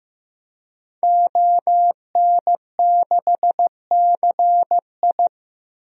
Morse code, heard as a keyed tone, about 15 wpm, 700 Hz. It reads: ON6CI
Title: DXCC-100 Call Signs - 5 Characters - 15wpm